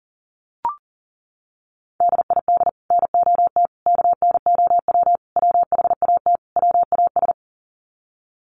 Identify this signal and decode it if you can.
Morse code, heard as a keyed tone, about 40 wpm, 700 Hz. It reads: DID NOT KNOW WHAT WAS
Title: Sets of 5 Words Encore2 40wpm